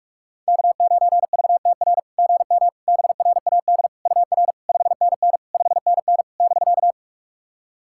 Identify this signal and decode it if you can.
Morse code, heard as a keyed tone, about 48 wpm, 690 Hz. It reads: K9VTR GM BRAD UR 5NN 5NN <BK>
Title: SOTA QSO Practice 45wpm